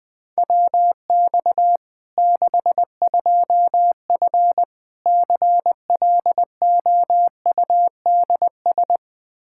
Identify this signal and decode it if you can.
Morse code, heard as a keyed tone, about 21 wpm, 700 Hz. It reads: WX 62F CLOUDS
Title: K1USN SST QSO Practice 20wpm